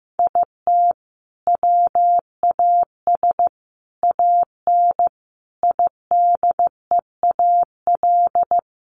Morse code, heard as a keyed tone, about 15 wpm, 700 Hz. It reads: IT WAS AN IDEAL
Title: Sentences from Top 1400 Words 15wpm